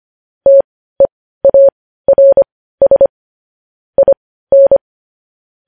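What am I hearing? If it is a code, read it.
Morse code, heard as a keyed tone, about 25 wpm, spaced as for 15 wpm, 550 Hz. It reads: TEARS IN
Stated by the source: Morse code news headlines sent at 15 words per minute.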